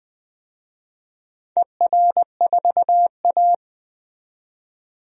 Morse code, heard as a keyed tone, about 20 wpm, 700 Hz. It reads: ER4A
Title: DXCC-100 Call Signs - 4 Characters - 20wpm